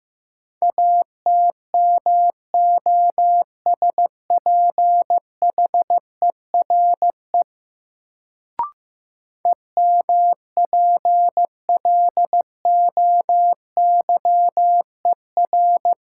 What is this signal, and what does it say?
Morse code, heard as a keyed tone, about 15 wpm, 700 Hz. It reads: ATMOSPHERE EMPLOYER
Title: Sentences from Top 1600 Words - New Word Drill Alt - 15wpm